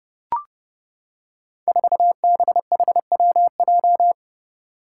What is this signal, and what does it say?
Morse code, heard as a keyed tone, about 30 wpm, 700 Hz. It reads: 4BHWJ
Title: Five Challenging Letter-Number Combinations 30wpm